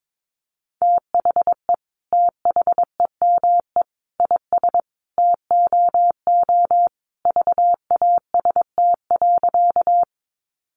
Morse code, heard as a keyed tone, about 22 wpm, 700 Hz. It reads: THE THEME IS TOO VAST.
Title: Sentences from Top 1300 Words 22wpm